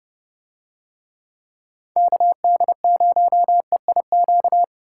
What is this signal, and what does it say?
Morse code, heard as a keyed tone, about 28 wpm, 700 Hz. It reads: KD0EIQ